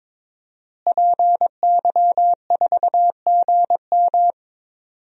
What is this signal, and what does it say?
Morse code, heard as a keyed tone, about 22 wpm, 700 Hz. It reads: PY4GM
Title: DXCC-100 Call Signs - 5 Characters - Repeated 3x - 22wpm